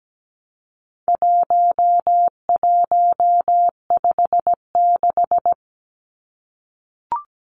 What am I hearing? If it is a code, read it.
Morse code, heard as a keyed tone, about 17 wpm, 700 Hz. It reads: 1156
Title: Numbers - 4-Digit 17wpm